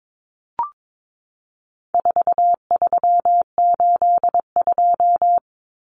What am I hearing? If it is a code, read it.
Morse code, heard as a keyed tone, about 22 wpm, 700 Hz. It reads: 4382
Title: Common Cut Numbers - 4-Digit 22wpm